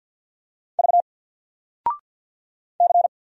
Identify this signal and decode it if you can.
Morse code, heard as a keyed tone, about 50 wpm, 700 Hz. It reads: V C
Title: Single Letter-Number 50wpm